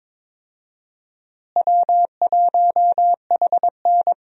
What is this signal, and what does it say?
Morse code, heard as a keyed tone, about 22 wpm, 700 Hz. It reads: W1HN
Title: USA Call Signs - 4 Characters - Repeated 3x - 22wpm